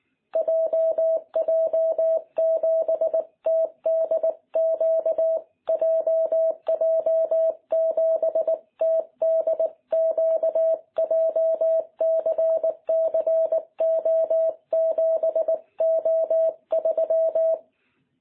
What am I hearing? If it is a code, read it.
Morse code, heard as a keyed tone, about 19 wpm, 650 Hz. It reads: JJ7TDQ JJ7TDQ JCCO7O3
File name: jh8cbh / 2021cw